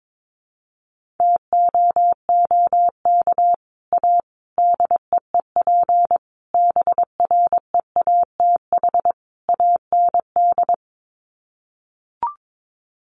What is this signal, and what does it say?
Morse code, heard as a keyed tone, about 22 wpm, 700 Hz. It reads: TOOK A DEEP BREATH AND
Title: Sets of 5 Words Encore 22wpm